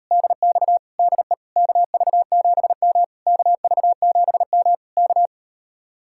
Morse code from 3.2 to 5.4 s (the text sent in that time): KV7M K